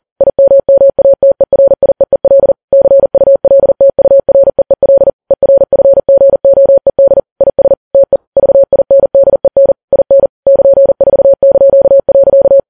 Morse code, heard as a keyed tone, about 40 wpm, 550 Hz. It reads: IMMATERIEEL CULTUREEL ERFGOED IS TE VINDEN IN <KN>4).
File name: Files Web tls / Morse on9fv / morse40kps